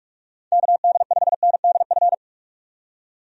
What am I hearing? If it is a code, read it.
Morse code, heard as a keyed tone, about 44 wpm, 700 Hz. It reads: KD5NDF
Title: USA Call Signs - 6 Characters - 45wpm